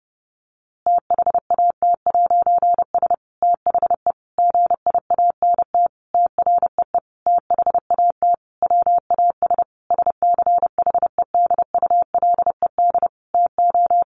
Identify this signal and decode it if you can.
Morse code, heard as a keyed tone, about 30 wpm, 700 Hz. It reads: THAT'S THE GIANT TREE THAT WAS SCHEDULED TO
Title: A Sound Of Thunder 30wpm